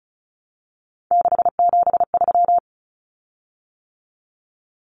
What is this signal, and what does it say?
Morse code, heard as a keyed tone, about 34 wpm, 700 Hz. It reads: 673
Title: Common Cut Numbers - 3-Digit 35wpm